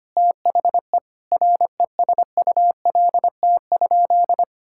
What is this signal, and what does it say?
Morse code, heard as a keyed tone, about 25 wpm, 700 Hz. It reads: THE RESULT?
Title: Sentences from Top 1000 Words - Review Part 5 - 25wpm